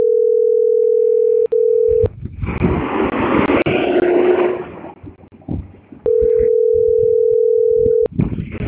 Ringback tone (North America), 440 + 480 Hz.